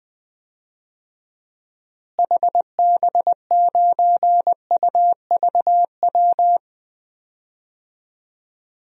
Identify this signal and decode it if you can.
Morse code, heard as a keyed tone, about 20 wpm, 700 Hz. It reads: HB9UVW
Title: DXCC-100 Call Signs - 6 Characters - Repeated 2x - 20wpm